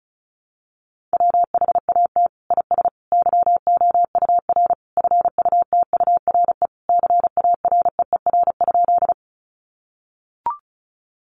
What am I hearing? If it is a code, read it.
Morse code, heard as a keyed tone, about 34 wpm, 700 Hz. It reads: WHAT IS YOUR FUTURE CAREER?
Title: Sentences from Top 1000 Words 35wpm